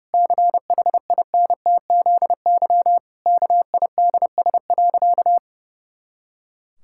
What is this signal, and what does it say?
Morse code, heard as a keyed tone, about 30 wpm, 700 Hz. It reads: CHINTZY KIDS.
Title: Pangram Challenge v2 - 30wpm